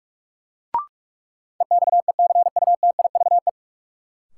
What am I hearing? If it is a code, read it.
Morse code, heard as a keyed tone, about 44 wpm, 700 Hz. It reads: EXECUTIVE
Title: Sets of 5 Words Encore - New Word Drill - 45wpm